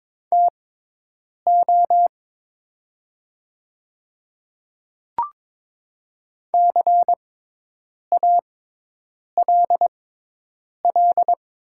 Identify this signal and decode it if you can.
Morse code, heard as a keyed tone, about 22 wpm, spaced as for 8 wpm, 700 Hz. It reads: TO CALL
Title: 6x Character Spacing - Top 100 Words - 22wpm